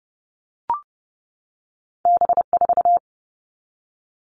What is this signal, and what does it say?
Morse code, heard as a keyed tone, about 30 wpm, 700 Hz. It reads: B4
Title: All Cut Numbers - 2-Digit 30wpm